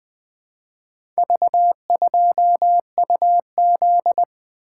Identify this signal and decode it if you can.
Morse code, heard as a keyed tone, about 20 wpm, 700 Hz. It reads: V2UZ
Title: Four Letter-Number Combinations - 20wpm